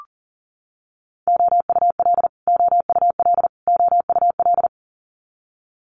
Morse code, heard as a keyed tone, about 40 wpm, 700 Hz. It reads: OUL OUL OUL
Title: Multi-letter Phonemes Repeated 3x 40wpm